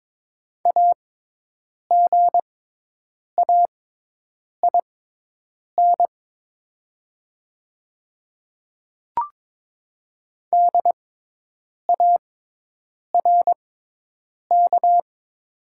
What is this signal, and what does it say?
Morse code, heard as a keyed tone, about 22 wpm, spaced as for 8 wpm, 700 Hz. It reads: AGAIN DARK
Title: 6x Character Spacing - Top 500 Words - 22wpm